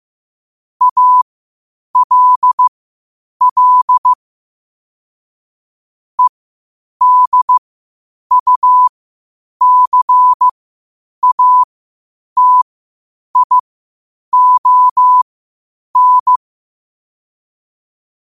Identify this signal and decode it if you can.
Morse code, heard as a keyed tone, about 15 wpm, spaced as for 8 wpm, 1000 Hz. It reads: ALL EDUCATION
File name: qotdfiles / 8wpm